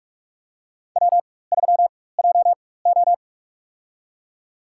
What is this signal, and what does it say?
Morse code, heard as a keyed tone, about 45 wpm, spaced as for 21 wpm, 700 Hz. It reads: W3JO
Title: USA Call Signs - 4 Characters - 4x Character Spacing - 45wpm